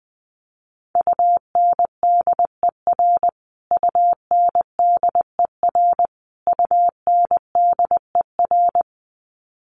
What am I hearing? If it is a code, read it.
Morse code, heard as a keyed tone, about 20 wpm, 700 Hz. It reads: UNDER UNDER UNDER
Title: Top 200 Words Repeated 20wpm